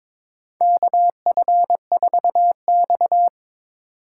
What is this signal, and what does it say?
Morse code, heard as a keyed tone, about 22 wpm, 700 Hz. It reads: KF4X
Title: USA Call Signs - 4 Characters - 3x Character Spacing - 22wpm